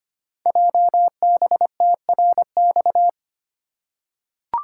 Morse code, heard as a keyed tone, about 25 wpm, 700 Hz. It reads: JBTRX
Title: Code Groups - 5 Letters - 25wpm